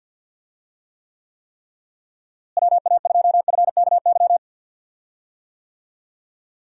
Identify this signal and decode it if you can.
Morse code, heard as a keyed tone, about 45 wpm, 700 Hz. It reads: WA2FKY
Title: USA Call Signs - 6 Characters - Repeated 3x - 50wpm